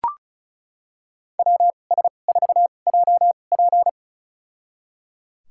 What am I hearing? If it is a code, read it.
Morse code, heard as a keyed tone, about 36 wpm, spaced as for 25 wpm, 700 Hz. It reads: WS4JP
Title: USA Call Signs - 5 Characters - 2x Character Spacing - 35wpm